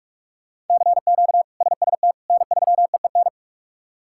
Morse code, heard as a keyed tone, about 46 wpm, 700 Hz. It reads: CQ SST N3EEN
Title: K1USN SST QSO Practice - 45wpm